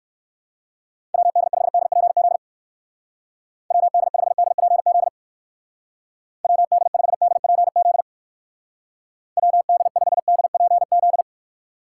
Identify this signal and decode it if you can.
Morse code, heard as a keyed tone, about 55 wpm, 700 Hz. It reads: WD5DPZ WD5DPZ WD5DPZ WD5DPZ
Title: Speed Racing - USA 2x3 Call Signs 45wpm